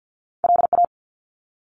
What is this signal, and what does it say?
Morse code, heard as a keyed tone, about 50 wpm, 700 Hz. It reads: LA
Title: ARRL Sections 50wpm